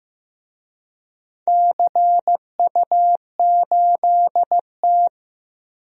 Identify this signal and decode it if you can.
Morse code, heard as a keyed tone, about 15 wpm, 700 Hz. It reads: CU8T
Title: DXCC-100 Call Signs - 4 Characters - Repeated 2x - 15wpm